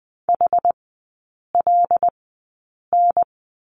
Morse code, heard as a keyed tone, about 21 wpm, spaced as for 8 wpm, 690 Hz. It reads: HLN
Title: 093 Lesson 16e - Letter c - Three Characters 20wpm